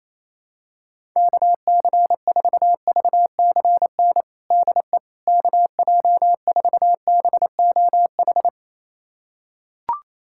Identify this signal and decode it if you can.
Morse code, heard as a keyed tone, about 28 wpm, 700 Hz. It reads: KC4VCN DE KJ4BOH
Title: USA Callsign DE Callsign - 6 Characters - 28wpm